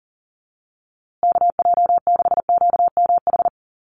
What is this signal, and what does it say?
Morse code, heard as a keyed tone, about 40 wpm, 700 Hz. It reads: KJ6QMH
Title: USA Call Signs Repeated 2x 40wpm